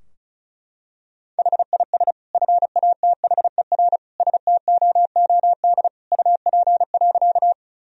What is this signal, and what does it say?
Morse code, heard as a keyed tone, about 34 wpm, 700 Hz. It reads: HIS FATHER STOOD UP.
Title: Sentences from Top 1000 Words - Review Part 2 - 35wpm